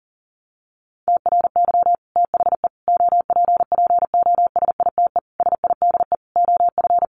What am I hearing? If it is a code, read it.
Morse code, heard as a keyed tone, about 40 wpm, 700 Hz. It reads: TRY THE OPPOSITE SIDE OF
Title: Sentences from Top 1300 Words 40wpm